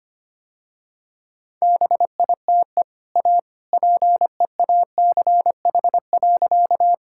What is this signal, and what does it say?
Morse code, heard as a keyed tone, about 25 wpm, 700 Hz. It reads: BITE A PEACH.
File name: Easy Sentences with Hints - 25wpm